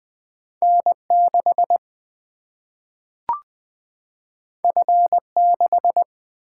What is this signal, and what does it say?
Morse code, heard as a keyed tone, about 20 wpm, 700 Hz. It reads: N6 F6
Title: Single-Letter Single-Number 20wpm